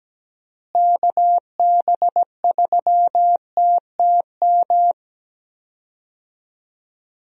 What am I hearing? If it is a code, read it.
Morse code, heard as a keyed tone, about 17 wpm, 700 Hz. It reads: KB3TTM
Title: USA Call Signs - 6 Characters - Repeated 3x - 17wpm